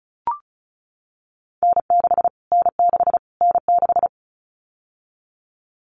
Morse code, heard as a keyed tone, about 34 wpm, 700 Hz. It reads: N6 N6 N6
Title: Callsign Prefixes for Top 25 DXCC - Repeated 3x - 35wpm